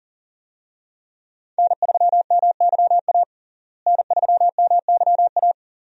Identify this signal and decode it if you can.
Morse code, heard as a keyed tone, about 37 wpm, 700 Hz. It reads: N3MYA N3MYA
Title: USA Call Signs - 5 Characters - Repeated 2x - 40wpm